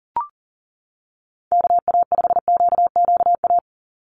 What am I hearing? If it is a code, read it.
Morse code, heard as a keyed tone, about 40 wpm, 700 Hz. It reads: KA5QQA
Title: NA SOTA Chaser Callsigns 40wpm